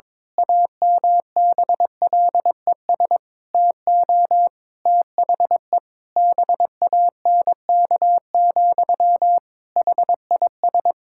Morse code, heard as a keyed tone, about 21 wpm, 700 Hz. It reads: AMBLES TO THE BANK, HIS
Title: Pangram Challenge v2 - 22wpm